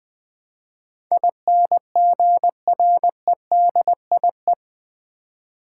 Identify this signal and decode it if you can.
Morse code, heard as a keyed tone, about 21 wpm, 700 Hz. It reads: INGREDIE
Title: Build A Word - 4 Letters At A Time - 10-Letter Words - 20wpm